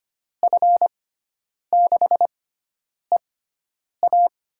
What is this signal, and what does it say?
Morse code, heard as a keyed tone, about 25 wpm, spaced as for 9 wpm, 700 Hz. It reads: F6EA
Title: USA Call Signs - 5 Characters - 6x Character Spacing - 25wpm